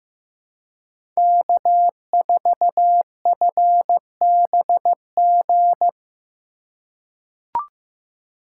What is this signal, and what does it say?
Morse code, heard as a keyed tone, about 15 wpm, 700 Hz. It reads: K4FBG
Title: USA Call Signs - 5 Characters - 3x Character Spacing - 15wpm